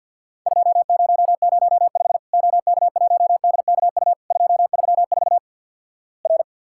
Morse code, heard as a keyed tone, about 50 wpm, 700 Hz. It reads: 2005 OK1DKU 234 R
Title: Worked All Europe DX Contest - QTC Exchange - 50wpm